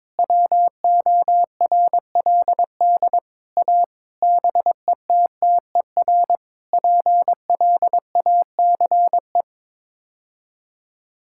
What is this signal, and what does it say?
Morse code, heard as a keyed tone, about 22 wpm, 700 Hz. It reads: WORLD A BETTER PLACE
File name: practice / 22wpm / Sets of 7 Words - 22wpm